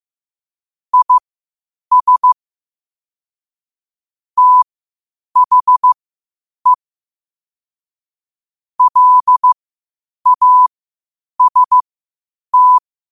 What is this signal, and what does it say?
Morse code, heard as a keyed tone, about 15 wpm, spaced as for 8 wpm, 1000 Hz. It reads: IS THE LAST
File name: qotdfiles / 8wpm